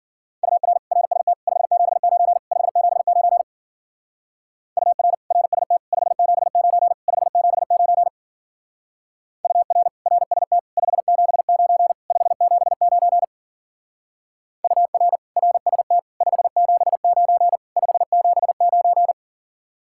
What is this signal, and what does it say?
Morse code, heard as a keyed tone, about 48 wpm, 700 Hz. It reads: UR RST 579 579 UR RST 579 579 UR RST 579 579 UR RST 579 579
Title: Speed-Racing Common QSO Phrases - 40wpm